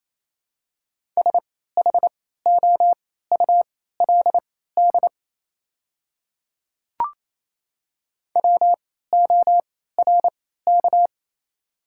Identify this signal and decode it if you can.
Morse code, heard as a keyed tone, about 28 wpm, spaced as for 16 wpm, 700 Hz. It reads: SHOULD WORK
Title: 3x Character Spacing - Top 300 Words - 28wpm